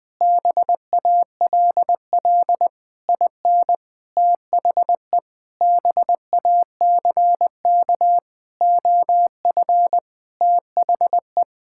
Morse code, heard as a keyed tone, about 20 wpm, 700 Hz. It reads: BALL IN THE BACK OF THE
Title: Sentences with 8-Word Phrases - 20wpm